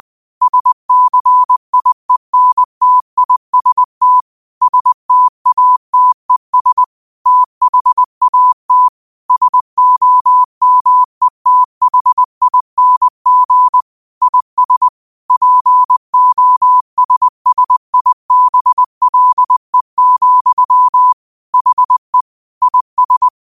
Morse code, heard as a keyed tone, about 20 wpm, 1000 Hz. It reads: SCIENTIST STATES THAT SOMETHING IS POSSIBLE, HE IS